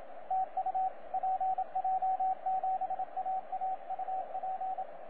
Morse code, heard as a keyed tone, about 26 wpm, 760 Hz, with radio noise. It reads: TU PJ7AA UP
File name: PJ7AA 17CW